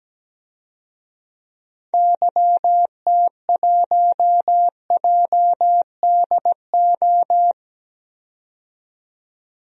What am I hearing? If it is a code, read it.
Morse code, heard as a keyed tone, about 17 wpm, 700 Hz. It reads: YT1JDO
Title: DXCC-100 Call Signs - 6 Characters - 17wpm